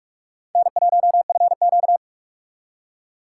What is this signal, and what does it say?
Morse code, heard as a keyed tone, about 56 wpm, 700 Hz. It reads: N1FQ